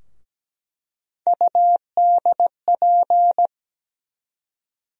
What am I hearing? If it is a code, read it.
Morse code, heard as a keyed tone, about 17 wpm, 700 Hz. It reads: UDP
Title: USA Call Sign Postfixes - 3 Letters 17wpm